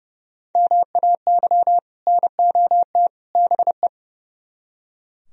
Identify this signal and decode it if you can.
Morse code, heard as a keyed tone, about 40 wpm, 700 Hz. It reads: MAY NOT BE